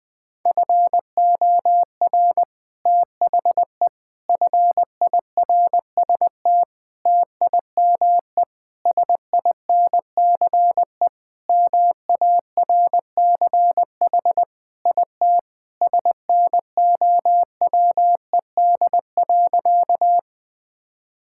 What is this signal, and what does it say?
Morse code, heard as a keyed tone, about 20 wpm, 700 Hz. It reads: FOR THE FIRST TIME SINCE MARCH IT SNOWED.
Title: Sentences with 5-Word Phrases - 20wpm